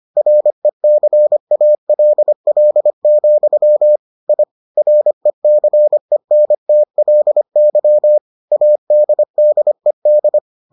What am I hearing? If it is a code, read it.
Morse code, heard as a keyed tone, about 25 wpm, 600 Hz. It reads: RECALL, I RECENTLY ADDED